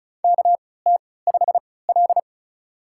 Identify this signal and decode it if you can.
Morse code, heard as a keyed tone, about 36 wpm, spaced as for 20 wpm, 700 Hz. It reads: KT5L
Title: USA Call Signs - 4 Characters - 3x Character Spacing - 35wpm